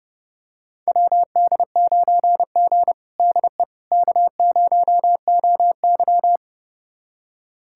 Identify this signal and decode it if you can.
Morse code, heard as a keyed tone, about 31 wpm, 700 Hz. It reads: WD9G DE K0OY
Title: USA Callsign DE Callsign - 4 Characters - 30wpm